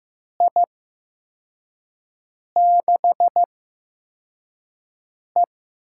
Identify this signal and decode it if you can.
Morse code, heard as a keyed tone, about 15 wpm, spaced as for 4 wpm, 700 Hz. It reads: I6E
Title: USA Call Signs - 8x Character Spacing - 15wpm